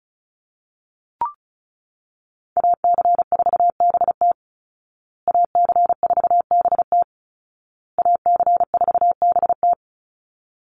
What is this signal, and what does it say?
Morse code, heard as a keyed tone, about 34 wpm, 700 Hz. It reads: AC4BT AC4BT AC4BT
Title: Call Signs Repeated 2x - 35wpm